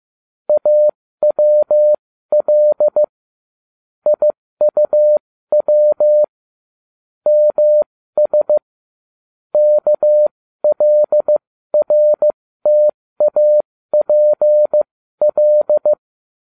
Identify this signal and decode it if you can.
Morse code, heard as a keyed tone, about 15 wpm, spaced as for 13 wpm, 600 Hz. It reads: AWL IUW MS KLRTAPL